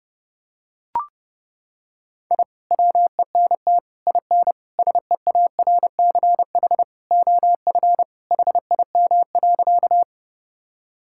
Morse code, heard as a keyed tone, about 32 wpm, 700 Hz. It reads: I WENT IN SEARCH OF HIM.
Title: Sentences from Top 1000 Words - Review Part 5 - 30wpm